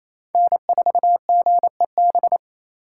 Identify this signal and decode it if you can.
Morse code, heard as a keyed tone, about 34 wpm, 700 Hz. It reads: N4GEB
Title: Speed Racing - USA Call Signs 28wpm